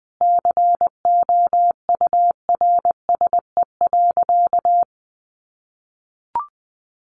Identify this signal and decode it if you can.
Morse code, heard as a keyed tone, about 20 wpm, 700 Hz. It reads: COURSE.
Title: Sentences from Top 500 Words 20wpm